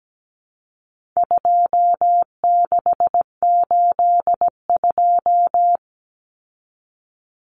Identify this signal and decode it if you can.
Morse code, heard as a keyed tone, about 17 wpm, 700 Hz. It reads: 2682
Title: Common Cut Numbers - 4-Digit 17wpm